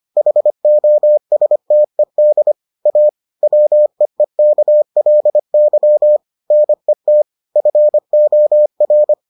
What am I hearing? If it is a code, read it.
Morse code, heard as a keyed tone, about 25 wpm, 600 Hz. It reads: HOSTED A WEEKLY NET FOR